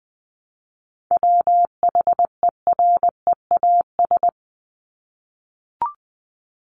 Morse code, heard as a keyed tone, about 20 wpm, 700 Hz. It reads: WHEREAS
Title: Top 1501-1750 Words 20wpm